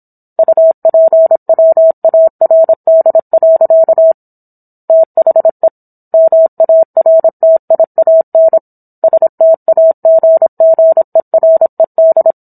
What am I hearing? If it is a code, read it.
Morse code, heard as a keyed tone, about 26 wpm, 650 Hz. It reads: UPWARD. THE MARTIAN STAGGERED